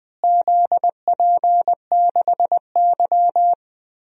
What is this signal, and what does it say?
Morse code, heard as a keyed tone, about 20 wpm, 700 Hz. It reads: ZP6Y
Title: DXCC-100 Call Signs - 4 Characters - Repeated 2x - 20wpm